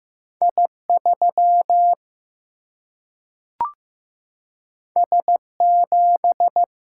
Morse code, heard as a keyed tone, about 15 wpm, 700 Hz. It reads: I3 S7
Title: Single-Letter Single-Number 15wpm